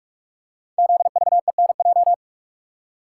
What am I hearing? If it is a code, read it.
Morse code, heard as a keyed tone, about 44 wpm, 690 Hz. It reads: ZVENJ
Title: Code Groups - 5 Letters - 45wpm